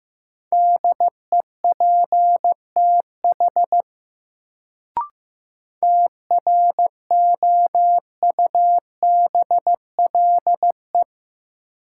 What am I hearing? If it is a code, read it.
Morse code, heard as a keyed tone, about 15 wpm, 700 Hz. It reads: DEPTH TROUBLE
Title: Sentences from Top 1200 Words - New Word Drill - 15wpm